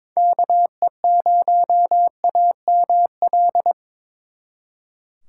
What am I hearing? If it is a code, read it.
Morse code, heard as a keyed tone, about 26 wpm, 700 Hz. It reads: KE0AML
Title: Speed Racing - USA 2x3 Call Signs 22wpm